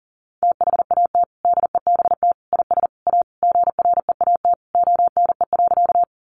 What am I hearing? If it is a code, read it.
Morse code, heard as a keyed tone, about 40 wpm, 700 Hz. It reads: THAT DEBT IS A GREAT ONE.
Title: Sentences from Top 1300 Words 40wpm